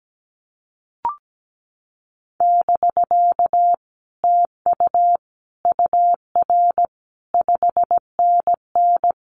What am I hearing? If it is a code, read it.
Morse code, heard as a keyed tone, about 17 wpm, 700 Hz. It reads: <BK> TU UR 5NN
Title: Common QSO Phrases Encore 17wpm